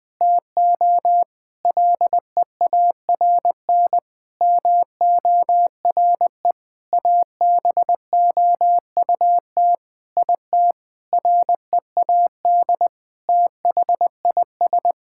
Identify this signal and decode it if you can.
Morse code, heard as a keyed tone, about 20 wpm, 700 Hz. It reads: TO LEARN MORE ABOUT IT READ THIS
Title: Sentences with 7-Word Phrases - Spoken Beginning - 20wpm